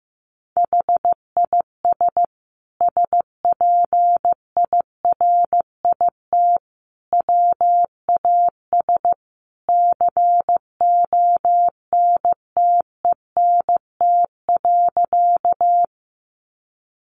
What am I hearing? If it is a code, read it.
Morse code, heard as a keyed tone, about 15 wpm, 700 Hz. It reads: HIS SPIRIT WAS CONTENT.
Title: Sentences from Top 1400 Words 15wpm